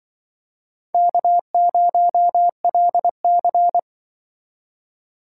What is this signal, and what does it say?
Morse code, heard as a keyed tone, about 23 wpm, 700 Hz. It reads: K0LC
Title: Speed Racing - USA 1x2 Call Signs 20wpm